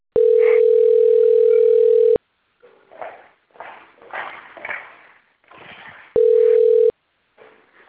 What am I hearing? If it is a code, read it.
Ringback tone (North America), 440 + 480 Hz.
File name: out- - - - - .59011